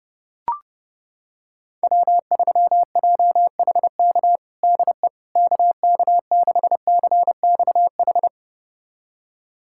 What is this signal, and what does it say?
Morse code, heard as a keyed tone, about 30 wpm, 700 Hz. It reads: W3JHK DE KK6CXH
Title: USA Callsign DE Callsign 30wpm